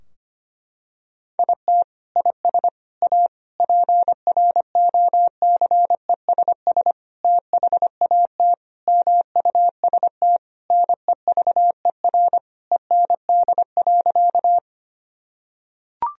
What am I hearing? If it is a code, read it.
Morse code, heard as a keyed tone, about 25 wpm, 700 Hz. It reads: IT IS A PROCESS THAT MUST NEVER END.
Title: Sentences from Top 1000 Words - Review Part 3 - 25wpm